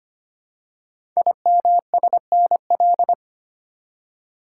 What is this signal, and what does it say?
Morse code, heard as a keyed tone, about 25 wpm, 700 Hz. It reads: IMSNL
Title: Code Groups - 5 Letters - 25wpm